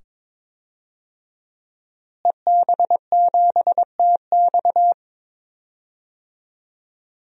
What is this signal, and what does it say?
Morse code, heard as a keyed tone, about 22 wpm, 700 Hz. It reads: EB7TX
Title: DXCC-100 Call Signs - 5 Characters - Repeated 2x - 22wpm